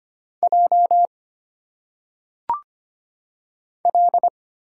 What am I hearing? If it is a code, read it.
Morse code, heard as a keyed tone, about 25 wpm, 700 Hz. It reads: J L
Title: Single Letters - Phonetic Pronunciation 25wpm